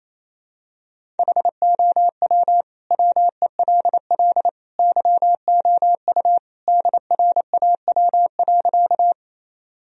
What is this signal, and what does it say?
Morse code, heard as a keyed tone, about 28 wpm, 700 Hz. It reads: HOW WELL YOU DRAW.
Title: Sentences from Top 1000 Words - Review Part 1 - 28wpm